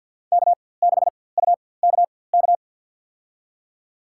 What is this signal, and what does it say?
Morse code, heard as a keyed tone, about 52 wpm, spaced as for 23 wpm, 700 Hz. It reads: K6UKK
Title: USA Call Signs - 5 Characters - 4x Character Spacing - 50wpm